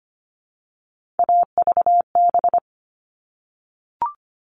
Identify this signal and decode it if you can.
Morse code, heard as a keyed tone, about 25 wpm, 700 Hz. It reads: AVB
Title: All Cut Numbers - 3-Digit 25wpm